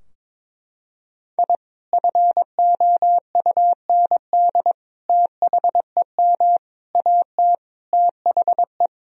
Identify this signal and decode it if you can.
Morse code, heard as a keyed tone, about 21 wpm, 700 Hz. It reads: I FOUND THEM AT THE
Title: Sentences from Top 1000 Words - Review Part 2 - 22wpm